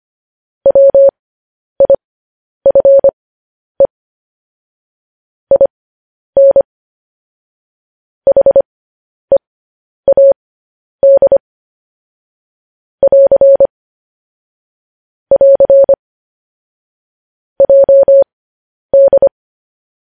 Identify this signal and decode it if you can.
Morse code, heard as a keyed tone, about 25 wpm, spaced as for 10 wpm, 550 Hz. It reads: WIFE IN HEAD <AR> <AR> JD
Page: news-headlines-in-morse-code-at-10-wpm